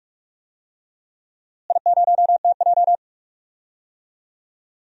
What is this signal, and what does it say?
Morse code, heard as a keyed tone, about 46 wpm, 700 Hz. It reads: I0TJ
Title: DXCC-100 Call Signs - 4 Characters - Repeated 3x - 45wpm